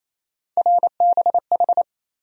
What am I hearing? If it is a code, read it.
Morse code, heard as a keyed tone, about 28 wpm, 700 Hz. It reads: RBH
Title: USA Call Sign Postfixes - 3 Letters 28wpm